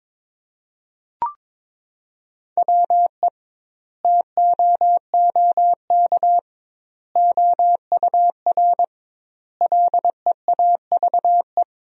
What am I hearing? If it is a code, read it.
Morse code, heard as a keyed tone, about 22 wpm, 700 Hz. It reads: WE TOOK OUR LEAVE
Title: 2x Word-Spacing - Sentences from Top 400 Words - 22wpm